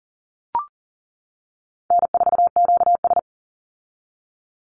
Morse code, heard as a keyed tone, about 40 wpm, 700 Hz. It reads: N4QS
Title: Easy Call Signs - 40wpm